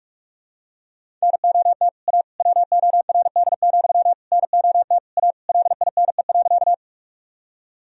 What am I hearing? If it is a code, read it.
Morse code, heard as a keyed tone, about 41 wpm, 700 Hz. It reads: NOT A WORD, NOT A LINE.
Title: Sentences from Top 1000 Words - Review Part 1 - 45wpm